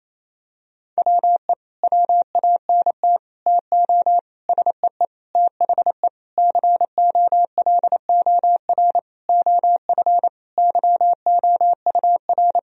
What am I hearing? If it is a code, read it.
Morse code, heard as a keyed tone, about 28 wpm, 700 Hz. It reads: WE WANT TO SEE THE COLOR OF YOUR
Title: Sentences from Top 1000 Words - Review Part 2 - 28wpm